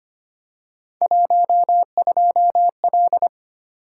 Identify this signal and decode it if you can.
Morse code, heard as a keyed tone, about 25 wpm, 700 Hz. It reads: 12L
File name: Three Letter-Number Combinations - 25wpm